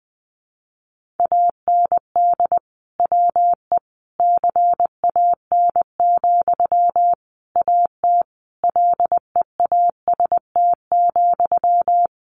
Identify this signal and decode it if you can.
Morse code, heard as a keyed tone, about 20 wpm, 700 Hz. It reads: AND WE CAN, AT LEAST,
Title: Sentences from Top 900 Words 20wpm